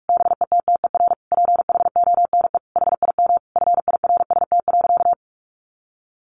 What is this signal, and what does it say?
Morse code, heard as a keyed tone, about 44 wpm, 700 Hz. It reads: BETTER PHONE HIM FIRST.
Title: Sentences from Top 1100 Words 45wpm